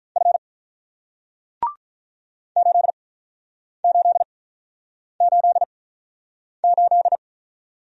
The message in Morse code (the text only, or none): F 8888